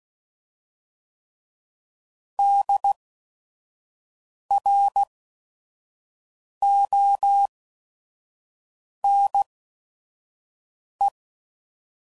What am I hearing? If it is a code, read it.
Morse code, heard as a keyed tone, about 16 wpm, spaced as for 5 wpm, 800 Hz. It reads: DRONE
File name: Prefixes / By First Letter / Ds